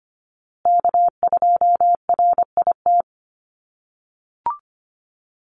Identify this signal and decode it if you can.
Morse code, heard as a keyed tone, about 25 wpm, 700 Hz. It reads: K2RIT
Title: USA Call Signs Repeated 2x 25wpm